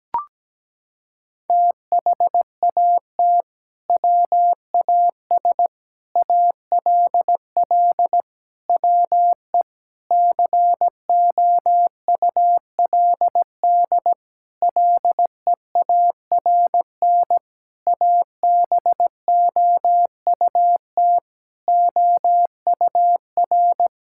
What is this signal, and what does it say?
Morse code, heard as a keyed tone, about 17 wpm, 700 Hz. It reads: THAT WAS ALL WE COULD LEARN ABOUT OUR
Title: Sentences from Top 1600 Words 17wpm